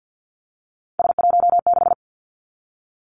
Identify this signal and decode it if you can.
Morse code, heard as a keyed tone, about 50 wpm, 700 Hz. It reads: S16
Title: Three Challenging Letter-Number Combinations 50wpm